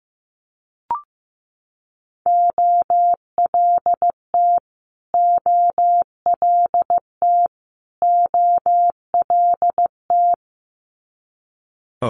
Morse code, heard as a keyed tone, about 15 wpm, 700 Hz. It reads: OLT OLT OLT
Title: Multi-letter Phonemes Repeated 3x 15wpm